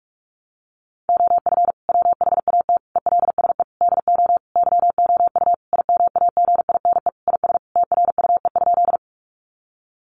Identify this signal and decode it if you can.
Morse code, heard as a keyed tone, about 44 wpm, 700 Hz. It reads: OF WHAT ELSE DO YOU IMAGINE IS TRUE?
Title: Sentences from Top 1400 Words 45wpm